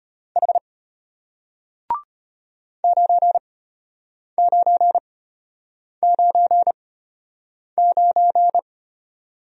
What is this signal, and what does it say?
Morse code, heard as a keyed tone, about 34 wpm, spaced as for 8 wpm, 700 Hz. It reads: H 9999